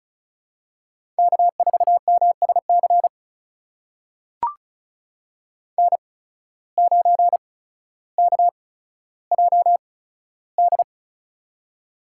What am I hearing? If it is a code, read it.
Morse code, heard as a keyed tone, about 36 wpm, spaced as for 15 wpm, 700 Hz. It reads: K4MSC N9KJD
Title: USA Call Signs - 5 Characters - 8x Character Spacing - 35wpm